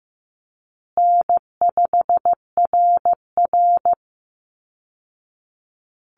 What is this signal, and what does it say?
Morse code, heard as a keyed tone, about 15 wpm, 700 Hz. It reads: N5RR
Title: Call Signs Repeated 2x - 15wpm